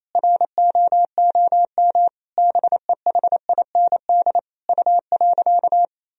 Morse code, heard as a keyed tone, about 28 wpm, 700 Hz. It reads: ROOM BEHIND U.
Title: Sentences from Top 500 Words and Abbreviations 28wpm